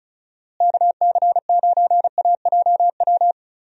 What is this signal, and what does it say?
Morse code, heard as a keyed tone, about 34 wpm, 700 Hz. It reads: KC9AJW
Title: USA Call Signs - 6 Characters - Repeated 3x - 35wpm